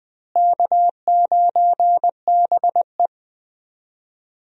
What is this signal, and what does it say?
Morse code, heard as a keyed tone, about 20 wpm, 700 Hz. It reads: K9BE
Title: USA Call Signs - 4 Characters - Repeated 3x - 20wpm